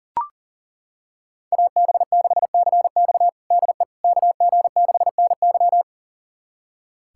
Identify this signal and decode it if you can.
Morse code, heard as a keyed tone, about 40 wpm, 700 Hz. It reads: AB6CX DE KG6NY
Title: USA Callsign DE Callsign - 5 Characters - 40wpm